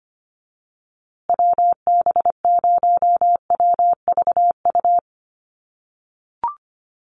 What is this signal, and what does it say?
Morse code, heard as a keyed tone, about 25 wpm, 700 Hz. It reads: WB0WVU
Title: USA Call Signs Repeated 2x 25wpm